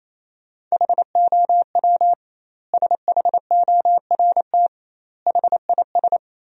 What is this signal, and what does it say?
Morse code, heard as a keyed tone, about 28 wpm, 700 Hz. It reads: HOW SHORT HIS
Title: 2x Word-Spacing - Sentences from Top 400 Words - 28wpm